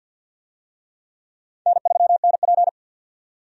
Morse code, heard as a keyed tone, about 52 wpm, 700 Hz. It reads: N3NP
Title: USA Call Signs - 4 Characters - 50wpm